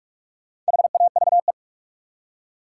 Morse code, heard as a keyed tone, about 54 wpm, 700 Hz. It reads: HAVE
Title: Speed-Racing Top 180 QSO Elements 45wpm